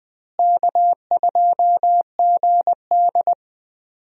Morse code, heard as a keyed tone, about 20 wpm, 700 Hz. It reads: K2GD
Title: USA Call Signs - 4 Characters - Repeated 2x - 20wpm